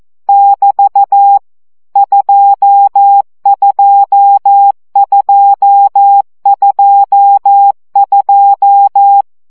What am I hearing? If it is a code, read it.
Morse code, heard as a keyed tone, about 14 wpm, 800 Hz. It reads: <BT> 22222
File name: CW gyakorlas 2 12 baud